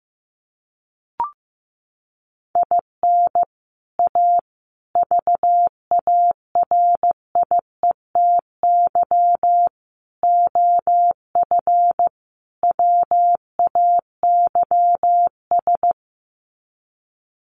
Morse code, heard as a keyed tone, about 15 wpm, 700 Hz. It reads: IN A VARIETY OF WAYS
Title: Sets of 5 Words Encore 15wpm